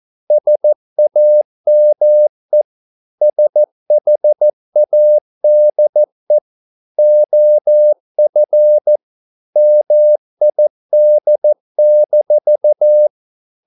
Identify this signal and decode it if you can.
Morse code, heard as a keyed tone, about 14 wpm, 600 Hz. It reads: SAME SHADE OF MID-